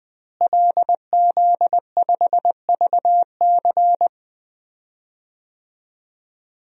Morse code, heard as a keyed tone, about 20 wpm, 700 Hz. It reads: LZ5VC
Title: DXCC-100 Call Signs - 5 Characters - Repeated 3x - 20wpm